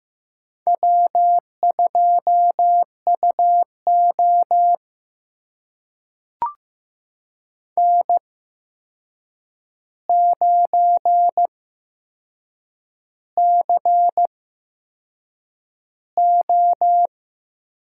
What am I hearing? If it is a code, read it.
Morse code, heard as a keyed tone, about 15 wpm, spaced as for 6 wpm, 700 Hz. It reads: W2UO N9CO
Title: USA Call Signs - 4 Characters - 8x Character Spacing - 15wpm